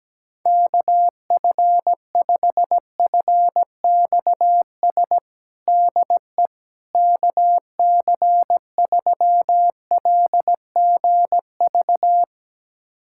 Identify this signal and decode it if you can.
Morse code, heard as a keyed tone, about 17 wpm, 700 Hz. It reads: KF5FXS DE KC3LGV
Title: USA Callsign DE Callsign - 6 Characters - 17wpm